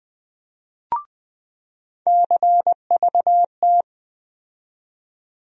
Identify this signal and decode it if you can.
Morse code, heard as a keyed tone, about 20 wpm, 700 Hz. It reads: CVT
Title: USA Call Sign Postfixes - 3 Letters 20wpm